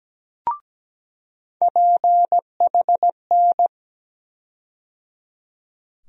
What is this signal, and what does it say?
Morse code, heard as a keyed tone, about 17 wpm, 700 Hz. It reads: PHN
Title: USA Call Sign Postfixes - 3 Letters 17wpm